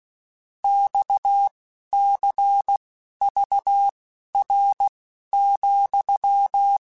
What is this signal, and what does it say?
Morse code, heard as a keyed tone, about 16 wpm, spaced as for 11 wpm, 800 Hz. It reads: XCVR,
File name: QSOs / QSOs-mp3s / WD0FBT 1@10